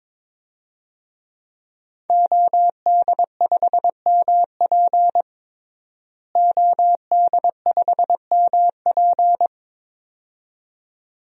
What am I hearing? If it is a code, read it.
Morse code, heard as a keyed tone, about 22 wpm, 700 Hz. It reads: OD5MP OD5MP
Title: DXCC-100 Call Signs - 5 Characters - Repeated 2x - 22wpm